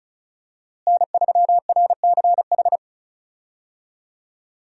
Morse code, heard as a keyed tone, about 34 wpm, 700 Hz. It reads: N3RCH
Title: USA Call Signs - 5 Characters - Repeated 3x - 35wpm